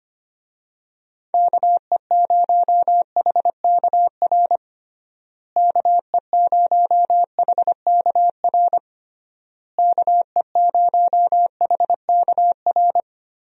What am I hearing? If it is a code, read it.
Morse code, heard as a keyed tone, about 25 wpm, 700 Hz. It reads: KE0HKR KE0HKR KE0HKR
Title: USA Call Signs - 6 Characters - Repeated 3x - 25wpm